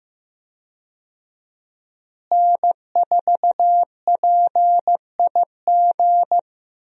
Morse code, heard as a keyed tone, about 15 wpm, 700 Hz. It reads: N4PIG
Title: USA Call Signs - 5 Characters - Repeated 2x - 15wpm